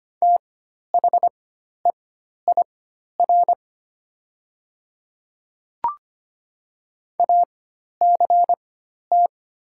Morse code, heard as a keyed tone, about 25 wpm, spaced as for 12 wpm, 700 Hz. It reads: THEIR ACT
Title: 4x Character Spacing - Top 200 Words - 25wpm